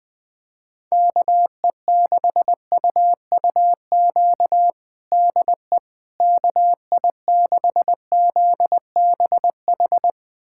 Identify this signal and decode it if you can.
Morse code, heard as a keyed tone, about 20 wpm, 700 Hz. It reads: KE6UUQ DE KI6ZBH
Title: USA Callsign DE Callsign 20wpm